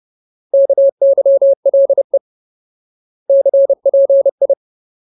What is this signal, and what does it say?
Morse code, heard as a keyed tone, about 30 wpm, 550 Hz. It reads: KYLE CPI
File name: Ultimate Word List / 30WPM Ultimate